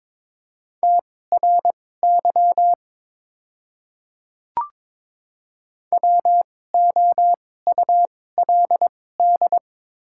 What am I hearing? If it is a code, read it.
Morse code, heard as a keyed tone, about 22 wpm, spaced as for 16 wpm, 690 Hz. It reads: TRY WOULD
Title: 2x Character Spacing - Top 300 Words - 22wpm